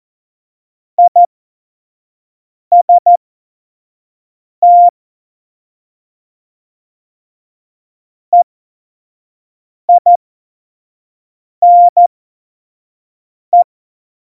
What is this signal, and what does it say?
Morse code, heard as a keyed tone, about 14 wpm, spaced as for 5 wpm, 700 Hz. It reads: IST EINE
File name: Morsekurs / Witze / Witz 11. Juni 2019, 25 F70